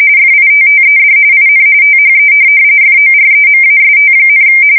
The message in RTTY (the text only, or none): FITTED IN THE FOREMAST,